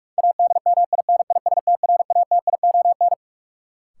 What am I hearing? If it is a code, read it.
Morse code, heard as a keyed tone, about 45 wpm, 700 Hz. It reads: ADMINISTRATION
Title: Sets of 5 Words - New Word Drill - 45wpm